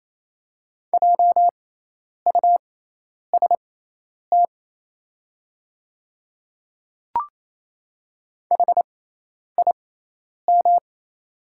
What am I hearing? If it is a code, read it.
Morse code, heard as a keyed tone, about 28 wpm, spaced as for 10 wpm, 690 Hz. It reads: JUST HIM
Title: 6x Character Spacing - Top 200 Words - 28wpm